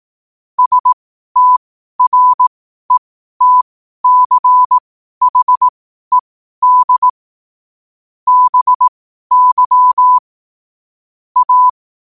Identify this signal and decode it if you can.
Morse code, heard as a keyed tone, about 18 wpm, spaced as for 12 wpm, 1000 Hz. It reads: STRETCHED BY A
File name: qotdfiles / 12wpm